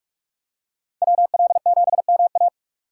Morse code, heard as a keyed tone, about 46 wpm, 700 Hz. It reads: WL7MA
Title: USA Call Signs - 5 Characters - 4x Character Spacing - 45wpm